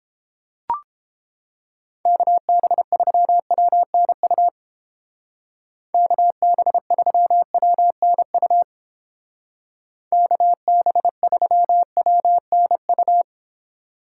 Morse code, heard as a keyed tone, about 31 wpm, 700 Hz. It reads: KB3WNU KB3WNU KB3WNU